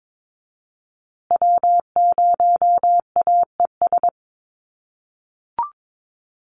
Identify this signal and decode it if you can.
Morse code, heard as a keyed tone, about 22 wpm, 700 Hz. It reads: W0AES
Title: USA Call Signs Repeated 3x 22wpm